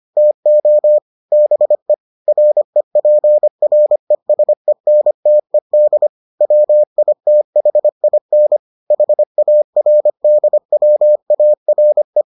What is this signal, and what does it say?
Morse code, heard as a keyed tone, about 25 wpm, 600 Hz. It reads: TO BE REPRESENTED WITHIN HARDWARE